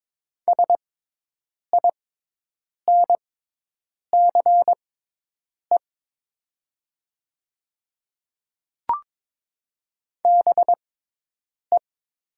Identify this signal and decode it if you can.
Morse code, heard as a keyed tone, about 22 wpm, spaced as for 8 wpm, 700 Hz. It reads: SINCE BE
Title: 6x Character Spacing - Top 500 Words - 22wpm